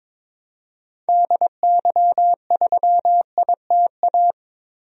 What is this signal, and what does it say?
Morse code, heard as a keyed tone, about 22 wpm, 700 Hz. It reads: DY3ITA
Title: DXCC-100 Call Signs - 6 Characters - Repeated 3x - 22wpm